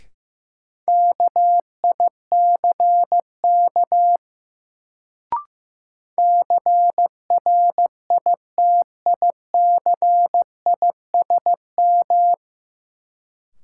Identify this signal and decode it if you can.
Morse code, heard as a keyed tone, about 15 wpm, 700 Hz. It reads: KICK CRITICISM
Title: Sentences from Top 1300 Words - New Word Drill Alt - 15wpm